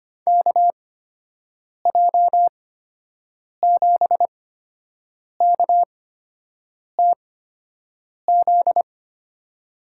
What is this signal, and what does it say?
Morse code, heard as a keyed tone, about 25 wpm, spaced as for 7 wpm, 700 Hz. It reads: KJ7KTZ
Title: USA Call Signs - 6 Characters - 8x Character Spacing - 25wpm